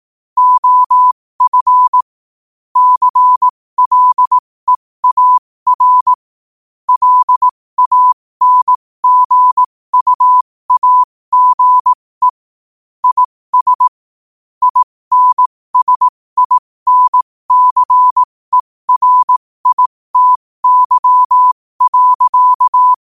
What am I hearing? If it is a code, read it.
Morse code, heard as a keyed tone, about 18 wpm, spaced as for 15 wpm, 1000 Hz. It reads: OF CLEAR LANGUAGE IS INSINCERITY.
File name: 15wpm